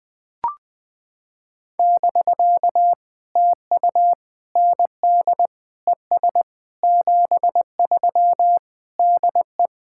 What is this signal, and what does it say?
Morse code, heard as a keyed tone, about 20 wpm, 700 Hz. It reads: <BK> TU ND ES 73 DE
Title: POTA QSO Practice - 20wpm